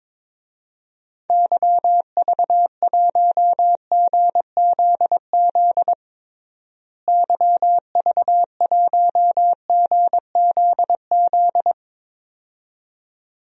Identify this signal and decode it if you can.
Morse code, heard as a keyed tone, about 22 wpm, 700 Hz. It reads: YV1GZZ YV1GZZ
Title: DXCC-100 Call Signs - 6 Characters - Repeated 2x - 22wpm